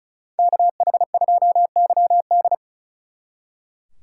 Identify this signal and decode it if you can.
Morse code, heard as a keyed tone, about 45 wpm, 700 Hz. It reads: KH2YD